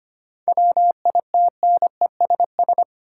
Morse code, heard as a keyed tone, about 25 wpm, 700 Hz. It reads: WITNESS
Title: Seven Letter Words - 25wpm